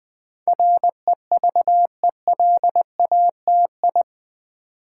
Morse code, heard as a keyed tone, about 20 wpm, 700 Hz. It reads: REVELATI
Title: Build A Word - 4 Letters At A Time - 10-Letter Words - 20wpm